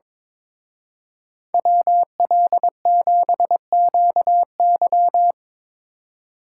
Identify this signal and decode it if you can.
Morse code, heard as a keyed tone, about 25 wpm, 700 Hz. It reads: WL7QY
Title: Speed Racing - USA 2x2 Call Signs 22wpm